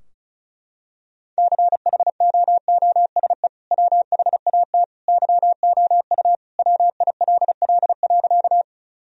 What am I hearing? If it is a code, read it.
Morse code, heard as a keyed tone, about 34 wpm, 700 Hz. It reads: CHOOSE WHAT YOU WILL.
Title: Sentences from Top 1000 Words - Review Part 4 - 35wpm